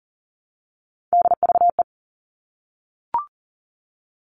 Morse code, heard as a keyed tone, about 40 wpm, 700 Hz. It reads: DVE
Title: All Cut Numbers - 3-Digit 40wpm